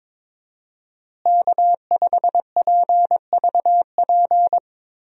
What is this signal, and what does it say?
Morse code, heard as a keyed tone, about 22 wpm, 700 Hz. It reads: K5PVP
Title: USA Call Signs - 5 Characters - Repeated 3x - 22wpm